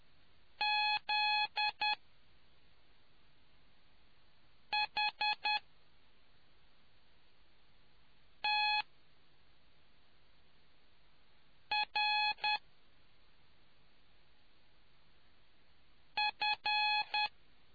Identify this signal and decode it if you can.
Morse code, heard as a keyed tone, about 10 wpm, spaced as for 3 wpm, 840 Hz. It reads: ZHTRF